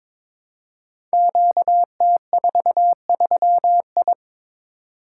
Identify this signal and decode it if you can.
Morse code, heard as a keyed tone, about 22 wpm, 700 Hz. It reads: QT43I
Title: Code Groups - 5 Letter:Numbers - 22wpm